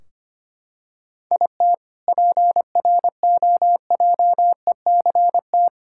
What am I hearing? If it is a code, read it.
Morse code, heard as a keyed tone, about 25 wpm, 700 Hz. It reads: IT PROJECT
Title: K1USN SST QSO Practice 25wpm